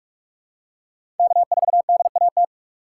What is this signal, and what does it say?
Morse code, heard as a keyed tone, about 46 wpm, 700 Hz. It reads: K4DAT
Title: USA Call Signs - 4x Character Spacing - 45wpm